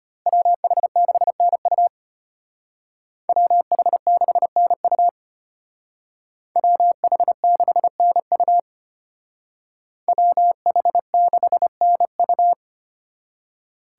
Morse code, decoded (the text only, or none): WH6NU WH6NU WH6NU WH6NU